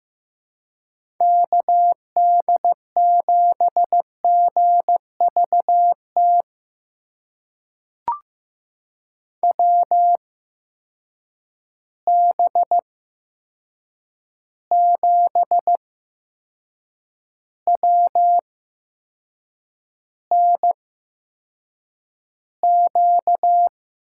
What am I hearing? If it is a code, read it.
Morse code, heard as a keyed tone, about 15 wpm, spaced as for 6 wpm, 700 Hz. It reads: KD7GVT WB7WNQ
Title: USA Call Signs - 6 Characters - 8x Character Spacing - 15wpm